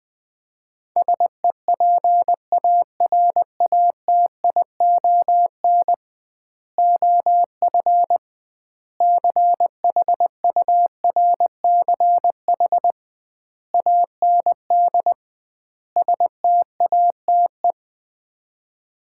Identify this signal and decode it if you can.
Morse code, heard as a keyed tone, about 20 wpm, 700 Hz. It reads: SEPARATION OF CHURCH AND STATE
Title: 2x Word-Spacing - Sets of 5 Words - 20wpm